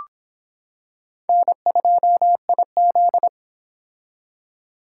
Morse code, heard as a keyed tone, about 26 wpm, 700 Hz. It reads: N2IZ